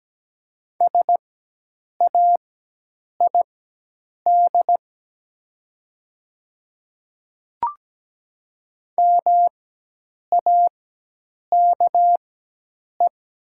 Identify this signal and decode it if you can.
Morse code, heard as a keyed tone, about 17 wpm, spaced as for 8 wpm, 700 Hz. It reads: SAID MAKE
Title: 4x Character Spacing - Top 200 Words - 17wpm